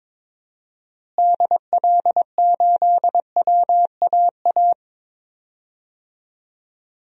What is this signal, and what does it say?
Morse code, heard as a keyed tone, about 21 wpm, 700 Hz. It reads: DL8WAA
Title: DXCC-100 Call Signs - 6 Characters - Repeated 2x - 22wpm